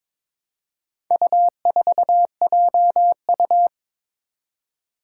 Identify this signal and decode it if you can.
Morse code, heard as a keyed tone, about 22 wpm, 700 Hz. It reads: U4JU
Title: Four Challenging Letter-Number Combinations 22wpm